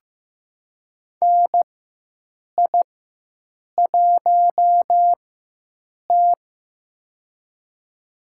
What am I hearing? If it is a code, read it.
Morse code, heard as a keyed tone, about 15 wpm, spaced as for 7 wpm, 700 Hz. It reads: NI1T
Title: USA Call Signs - 4 Characters - 4x Character Spacing - 15wpm